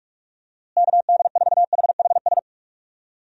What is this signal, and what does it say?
Morse code, heard as a keyed tone, about 46 wpm, 700 Hz. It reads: KD4HHS
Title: USA Call Signs - 6 Characters - Repeated 2x - 45wpm